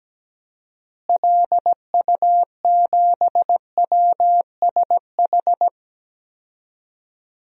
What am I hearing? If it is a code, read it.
Morse code, heard as a keyed tone, about 17 wpm, 700 Hz. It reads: LU7WSH
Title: DXCC-100 Call Signs - 6 Characters - Repeated 3x - 17wpm